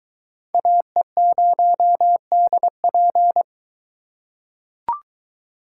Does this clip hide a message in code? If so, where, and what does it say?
Yes: Morse code from 0.5 to 3.4 s, heard as a keyed tone, about 22 wpm, 700 Hz. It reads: AE0DP